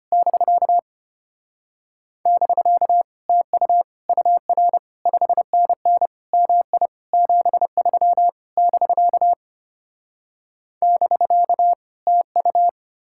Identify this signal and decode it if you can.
Morse code, heard as a keyed tone, about 33 wpm, 700 Hz. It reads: <BK> <BK> TU UR 5NN MI 73 <BK> <BK> TU